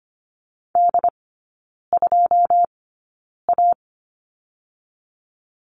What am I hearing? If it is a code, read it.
Morse code, heard as a keyed tone, about 24 wpm, 700 Hz. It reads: D 2 A
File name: course / 25:10wpm / 100 Lesson 17e - Letter u - Three Characters 25wpm